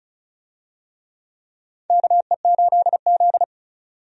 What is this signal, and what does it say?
Morse code, heard as a keyed tone, about 34 wpm, 700 Hz. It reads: KE8Z
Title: USA Call Signs - 4 Characters - 35wpm